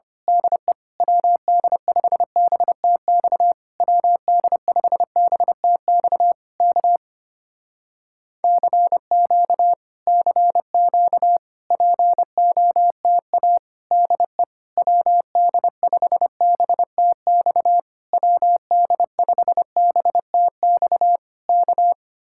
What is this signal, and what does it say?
Morse code, heard as a keyed tone, about 25 wpm, 700 Hz. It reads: DE WD5BTX WD5BTX K CQ CQ POTA DE WD5BTX WD5BTX K
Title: Speed-Racing Common QSO Phrases Encore - 25wpm